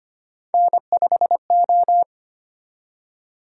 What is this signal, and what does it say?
Morse code, heard as a keyed tone, about 25 wpm, 700 Hz. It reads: N5O
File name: Three Letter-Number Combinations - 25wpm